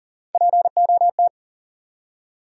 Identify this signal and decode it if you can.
Morse code, heard as a keyed tone, about 42 wpm, 700 Hz. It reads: POT
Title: Three Letter Words - 40wpm